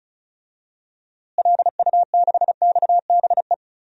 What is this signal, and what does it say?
Morse code, heard as a keyed tone, about 36 wpm, 700 Hz. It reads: LU6XBE
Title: DXCC-100 Call Signs - 6 Characters - Repeated 2x - 35wpm